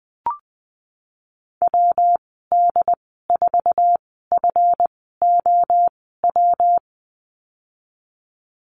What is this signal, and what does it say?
Morse code, heard as a keyed tone, about 19 wpm, spaced as for 14 wpm, 700 Hz. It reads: WD4FOW
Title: 161 Lesson 26g - Letter g - Call Signs 20wpm